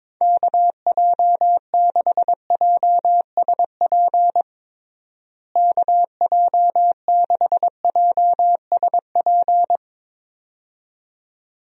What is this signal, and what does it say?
Morse code, heard as a keyed tone, about 21 wpm, 700 Hz. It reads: KJ6JSP KJ6JSP
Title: USA Call Signs - 6 Characters - Repeated 2x - 22wpm